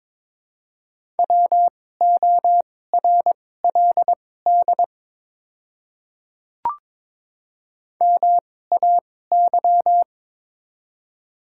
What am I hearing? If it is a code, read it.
Morse code, heard as a keyed tone, about 22 wpm, spaced as for 16 wpm, 700 Hz. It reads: WORLD MAY
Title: 2x Character Spacing - Top 300 Words - 22wpm